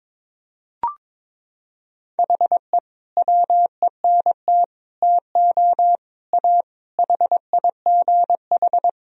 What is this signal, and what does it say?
Morse code, heard as a keyed tone, about 22 wpm, 700 Hz. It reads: HE WENT TO A HIGH
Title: Sentences from Top 1000 Words - Review Part 1 - 22wpm